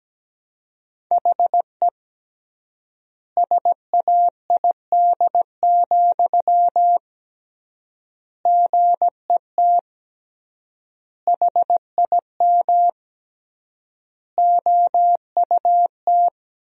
Morse code, heard as a keyed tone, about 17 wpm, 700 Hz. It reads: HE SAID, GET HIM OUT
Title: 3x Word-Spacing - Sentences from Top 400 Words - 17wpm